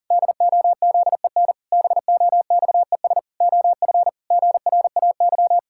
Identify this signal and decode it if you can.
Morse code, heard as a keyed tone, about 38 wpm, 700 Hz. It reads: DOZEN BOXES OF GRAY
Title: Pangram Challenge v2 - 40wpm